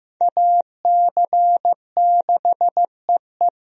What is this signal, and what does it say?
Morse code, heard as a keyed tone, about 15 wpm, 700 Hz. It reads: AC6EE
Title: USA Call Signs - 5 Characters - 15wpm